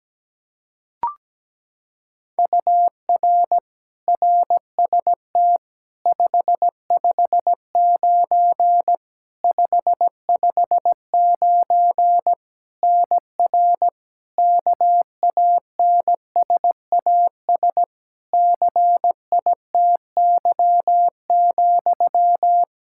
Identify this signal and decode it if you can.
Morse code, heard as a keyed tone, about 17 wpm, 700 Hz. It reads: UR RST 559 559 NR KANSAS CITY,
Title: K1USN SST QSO Practice 17wpm